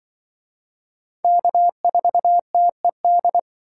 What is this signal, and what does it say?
Morse code, heard as a keyed tone, about 24 wpm, 700 Hz. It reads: K4TED
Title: Speed Racing - USA 1x3 Call Signs 20wpm